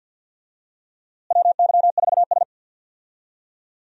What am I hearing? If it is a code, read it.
Morse code, heard as a keyed tone, about 46 wpm, 700 Hz. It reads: WY4S
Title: USA Call Signs - 4 Characters - Repeated 2x - 50wpm